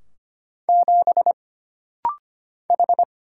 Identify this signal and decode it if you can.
Morse code, heard as a keyed tone, about 25 wpm, 700 Hz. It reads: 7 H
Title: Single Letter-Number - Rapid-Fire 25wpm